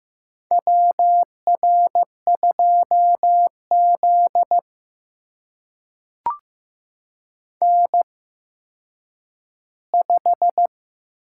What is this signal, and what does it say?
Morse code, heard as a keyed tone, about 15 wpm, spaced as for 9 wpm, 700 Hz. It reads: WR2Z N5
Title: USA Call Signs - 4 Characters - 8x Character Spacing - 15wpm